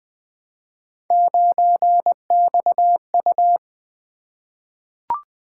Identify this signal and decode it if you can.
Morse code, heard as a keyed tone, about 20 wpm, 700 Hz. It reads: 9XU
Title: Three Letter-Number Combinations - 20wpm